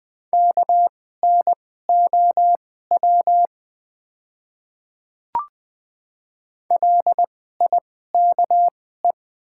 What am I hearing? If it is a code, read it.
Morse code, heard as a keyed tone, about 20 wpm, spaced as for 14 wpm, 700 Hz. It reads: KNOW LIKE
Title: 2x Character Spacing - Top 100 Words - 20wpm